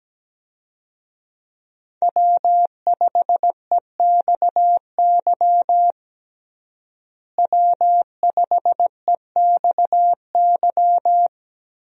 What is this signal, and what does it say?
Morse code, heard as a keyed tone, about 17 wpm, 700 Hz. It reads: W5EXY W5EXY
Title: USA Call Signs - 5 Characters - Repeated 2x - 17wpm